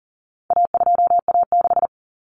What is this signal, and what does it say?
Morse code, heard as a keyed tone, about 40 wpm, 700 Hz. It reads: A2A6
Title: Common Cut Numbers - 4-Digit 40wpm